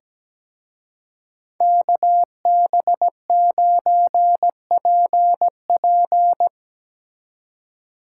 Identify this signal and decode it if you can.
Morse code, heard as a keyed tone, about 19 wpm, 700 Hz. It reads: KB9PP
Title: Speed Racing - USA 2x2 Call Signs 17wpm